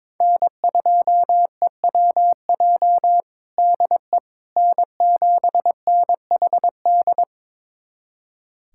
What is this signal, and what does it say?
Morse code, heard as a keyed tone, about 22 wpm, 700 Hz. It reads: N2EWJ DE N7NHD
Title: USA Callsign DE Callsign - 5 Characters - 22wpm